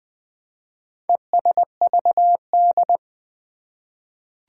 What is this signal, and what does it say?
Morse code, heard as a keyed tone, about 20 wpm, 700 Hz. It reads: ESVD
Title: Four Challenging Letter-Number Combinations 20wpm